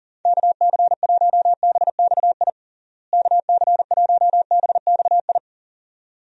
Morse code, heard as a keyed tone, about 40 wpm, 700 Hz. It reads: KC1BXI KC1BXI
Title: USA Call Signs - 6 Characters - Repeated 2x - 40wpm